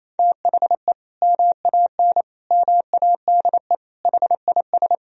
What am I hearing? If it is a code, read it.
Morse code, heard as a keyed tone, about 28 wpm, 700 Hz. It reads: THE MAN MADE HIS
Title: Sentences from Top 1000 Words - Review Part 2 - 28wpm